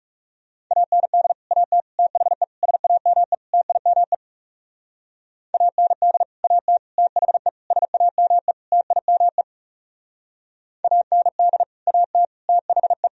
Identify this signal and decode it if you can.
Morse code, heard as a keyed tone, about 38 wpm, 700 Hz. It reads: AND AT THE SAME TIME AND AT THE SAME TIME AND AT THE